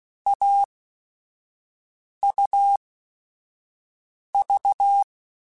Morse code, heard as a keyed tone, about 16 wpm, spaced as for 5 wpm, 800 Hz. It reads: AUV
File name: Short Runs / Short Runs-mp3s / AUV AUV4 @5